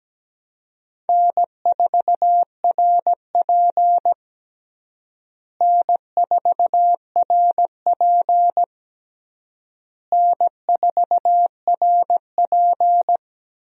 Morse code, heard as a keyed tone, about 17 wpm, 700 Hz. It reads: N4RP N4RP N4RP
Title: USA Call Signs - 4 Characters - Repeated 3x - 17wpm